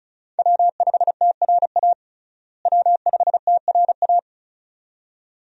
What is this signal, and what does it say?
Morse code, heard as a keyed tone, about 36 wpm, 700 Hz. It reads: W5TRA W5TRA
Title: USA Call Signs - 5 Characters - Repeated 2x - 35wpm